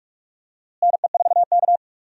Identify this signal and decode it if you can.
Morse code, heard as a keyed tone, about 46 wpm, 700 Hz. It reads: NE4K
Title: USA Call Signs - 4 Characters - Repeated 3x - 45wpm